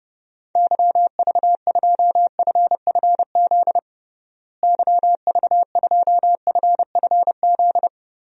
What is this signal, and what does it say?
Morse code, heard as a keyed tone, about 30 wpm, 700 Hz. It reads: YV2FFZ YV2FFZ
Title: DXCC-100 Call Signs - 6 Characters - Repeated 3x - 30wpm